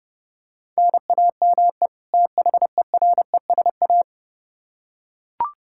Morse code, heard as a keyed tone, about 30 wpm, 700 Hz. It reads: NAME THERESA
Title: K1USN SST QSO Practice 30wpm